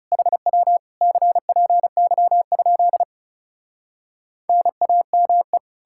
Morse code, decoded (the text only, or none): HW CPY? NAME